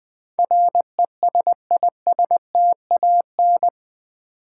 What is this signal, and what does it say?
Morse code, heard as a keyed tone, about 20 wpm, 700 Hz. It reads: RESISTAN
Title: Build A Word - 4 Letters At A Time - 10-Letter Words - 20wpm